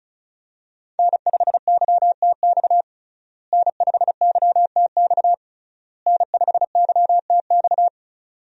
Morse code, heard as a keyed tone, about 34 wpm, 700 Hz. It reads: N5YTX N5YTX N5YTX
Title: USA Call Signs - 5 Characters - Repeated 3x - 35wpm